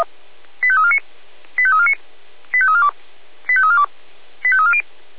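Five-tone selective call (ZVEI), heard as quick runs of five tones, tones of about 70 ms. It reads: 85349 85349 85342 85342 85340
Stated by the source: ZVEI five-tone selective call received via SDR.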